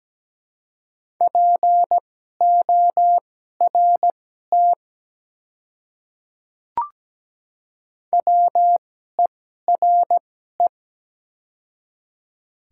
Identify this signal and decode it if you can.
Morse code, heard as a keyed tone, about 17 wpm, spaced as for 12 wpm, 700 Hz. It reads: PORT WERE
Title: 2x Character Spacing - Top 300 Words - 17wpm